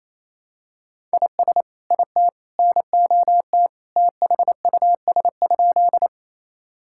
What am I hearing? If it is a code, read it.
Morse code, heard as a keyed tone, about 28 wpm, 700 Hz. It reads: IS IT NOT THUS?
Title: Sentences from Top 1000 Words - Review Part 5 - 28wpm